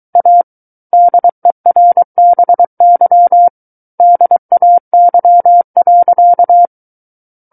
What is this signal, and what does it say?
Morse code, heard as a keyed tone, about 23 wpm, 700 Hz. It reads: A DERBY DAY.
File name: Book One / 14-Chapter 14 - 23 WPM